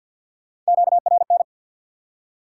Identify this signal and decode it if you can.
Morse code, heard as a keyed tone, about 69 wpm, spaced as for 59 wpm, 700 Hz. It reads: QRN
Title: Speed-Racing Top 180 QSO Elements 50wpm